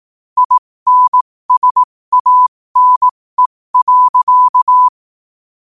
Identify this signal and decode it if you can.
Morse code, heard as a keyed tone, about 18 wpm, spaced as for 15 wpm, 1000 Hz. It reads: INSANE.
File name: qotdfiles / 15wpm